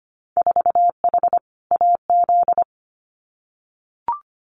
Morse code, heard as a keyed tone, about 25 wpm, 700 Hz. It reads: 4H AZ
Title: Winter Field Day Exchanges Encore 25wpm